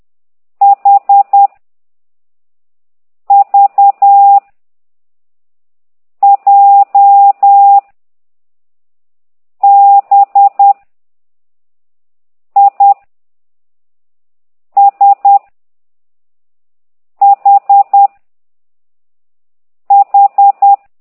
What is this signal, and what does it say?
Morse code, heard as a keyed tone, about 10 wpm, 800 Hz. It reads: H V J B I S H H
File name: ct2idw / exerc6